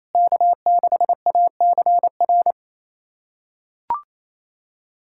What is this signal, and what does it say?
Morse code, heard as a keyed tone, about 28 wpm, 700 Hz. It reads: K6ACR
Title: USA Callsign DE Callsign - 5 Characters - 28wpm